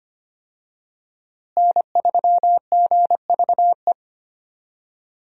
Morse code, heard as a keyed tone, about 25 wpm, 700 Hz. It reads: N3GVE
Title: USA Call Signs - 5 Characters - Repeated 3x - 25wpm